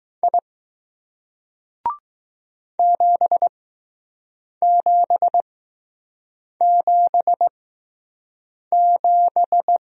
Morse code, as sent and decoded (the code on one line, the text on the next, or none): .. / --... --... --... --...
I 7777